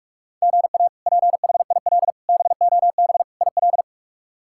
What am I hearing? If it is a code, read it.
Morse code, heard as a keyed tone, about 46 wpm, 700 Hz. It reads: GA PHIL BOB IL
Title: K1USN SST QSO Practice - 45wpm